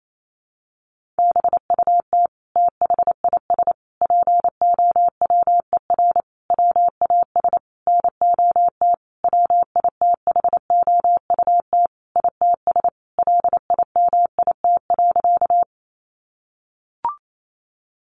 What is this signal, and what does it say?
Morse code, heard as a keyed tone, about 28 wpm, 700 Hz. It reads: BUT THIS POWER WAS NOT WITHOUT ITS LIMIT.
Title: Sentences from Top 1500 Words 28wpm